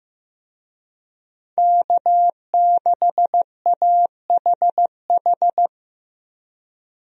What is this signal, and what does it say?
Morse code, heard as a keyed tone, about 15 wpm, 700 Hz. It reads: K6AHH
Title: USA Call Signs - 5 Characters - Repeated 3x - 15wpm